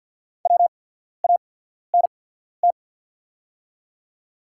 Morse code, heard as a keyed tone, about 48 wpm, spaced as for 14 wpm, 690 Hz. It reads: WANT
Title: 8x Character Spacing - Top 500 Words - 50wpm